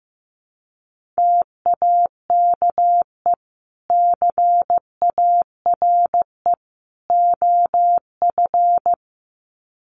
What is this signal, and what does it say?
Morse code, heard as a keyed tone, about 15 wpm, 700 Hz. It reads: TAKE CARE OF
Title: Sets of 3 Words 15wpm